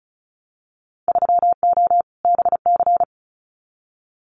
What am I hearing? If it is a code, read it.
Morse code, heard as a keyed tone, about 36 wpm, 700 Hz. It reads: 3O BC
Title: Winter Field Day Exchanges Encore 35wpm